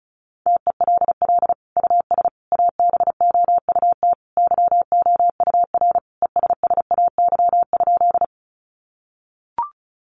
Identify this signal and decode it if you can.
Morse code, heard as a keyed tone, about 34 wpm, 700 Hz. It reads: TELL US ABOUT YOUR ESSAY?
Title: Sentences from Top 1500 Words 35wpm